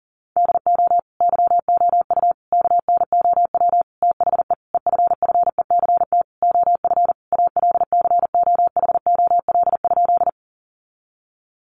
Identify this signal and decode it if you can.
Morse code, heard as a keyed tone, about 40 wpm, 700 Hz. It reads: DO YOU KNOW THE EFFECT OF ALCOHOL?
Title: Sentences from Top 1400 Words 40wpm